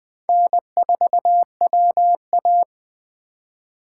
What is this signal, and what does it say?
Morse code, heard as a keyed tone, about 21 wpm, 700 Hz. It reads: N4WA
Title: Speed Racing - USA 1x2 Call Signs 17wpm